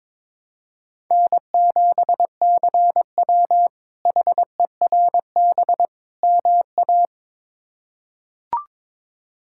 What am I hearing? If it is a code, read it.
Morse code, heard as a keyed tone, about 21 wpm, 700 Hz. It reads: N7CW HERB MA
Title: K1USN SST QSO Practice - 22wpm